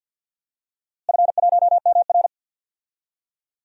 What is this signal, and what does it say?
Morse code, heard as a keyed tone, about 47 wpm, 700 Hz. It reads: F1MR
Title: DXCC-100 Call Signs - 4 Characters - Repeated 2x - 50wpm